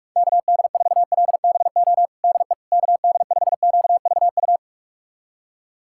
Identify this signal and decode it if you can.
Morse code, heard as a keyed tone, about 42 wpm, 690 Hz. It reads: KD4LBO DE KD5QVU
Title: USA Callsign DE Callsign - 6 Characters - 45wpm